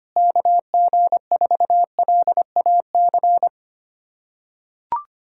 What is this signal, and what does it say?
Morse code, heard as a keyed tone, about 25 wpm, 700 Hz. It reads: KG4LAC
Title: USA Callsign DE Callsign - 6 Characters - 25wpm